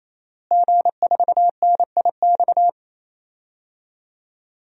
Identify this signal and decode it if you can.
Morse code, heard as a keyed tone, about 28 wpm, 700 Hz. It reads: G4NIX
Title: DXCC-100 Call Signs - 5 Characters - Repeated 3x - 28wpm